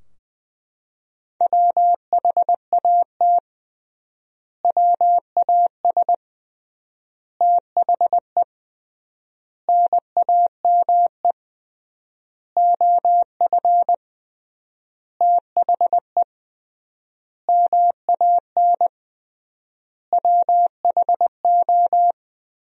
Morse code, heard as a keyed tone, about 20 wpm, 700 Hz. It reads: WHAT WAS THE NAME OF THE MAN WHO
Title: 3x Word-Spacing - Sentences from Top 300 Words - 20wpm